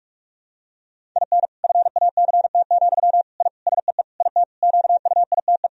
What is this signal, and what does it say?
Morse code, heard as a keyed tone, about 47 wpm, 700 Hz. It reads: IN FACT, I SEE IT QUITE
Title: Sentences from Top 1000 Words - Review Part 3 - 45wpm